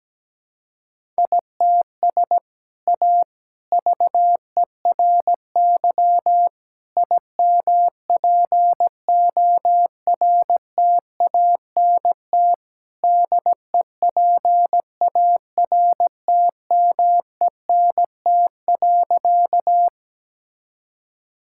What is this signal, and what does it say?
Morse code, heard as a keyed tone, about 17 wpm, 700 Hz. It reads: ITS A VERY IMPORTANT DEPARTMENT.
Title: Sentences from Top 1000 Words - Review Part 5 - 17wpm